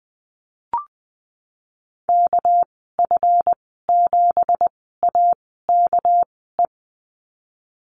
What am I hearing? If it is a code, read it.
Morse code, heard as a keyed tone, about 19 wpm, spaced as for 14 wpm, 700 Hz. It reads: KF7AKE
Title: 207 Lesson 33g - Letter j - Call Signs 20wpm